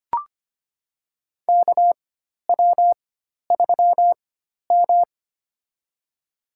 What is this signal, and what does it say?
Morse code, heard as a keyed tone, about 25 wpm, spaced as for 12 wpm, 700 Hz. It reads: KW3M
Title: USA Call Signs - 4 Characters - 4x Character Spacing - 25wpm